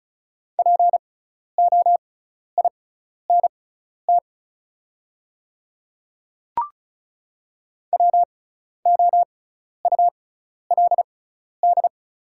Morse code, heard as a keyed tone, about 36 wpm, spaced as for 12 wpm, 700 Hz. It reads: POINT WOULD
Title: 6x Character Spacing - Top 200 Words - 35wpm